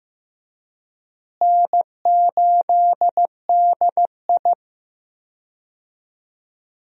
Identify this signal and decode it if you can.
Morse code, heard as a keyed tone, about 15 wpm, 700 Hz. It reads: N8DI
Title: USA Call Signs - 4 Characters - Repeated 3x - 15wpm